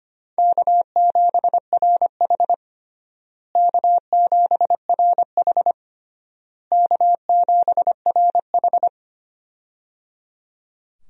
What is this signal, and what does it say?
Morse code, heard as a keyed tone, about 25 wpm, 700 Hz. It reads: K7RH K7RH K7RH
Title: USA Call Signs - 4 Characters - Repeated 3x - 25wpm